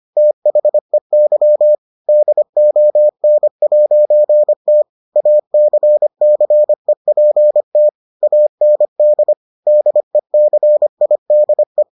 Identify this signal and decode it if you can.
Morse code, heard as a keyed tone, about 25 wpm, 600 Hz. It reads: THEY DON'T ACCEPT AND DECIDE